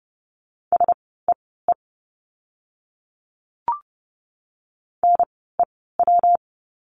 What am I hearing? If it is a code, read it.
Morse code, heard as a keyed tone, about 30 wpm, spaced as for 17 wpm, 700 Hz. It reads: SEE NEW
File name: course / 30:10wpm / 237 Lesson 38f - Letter q - Words 30wpm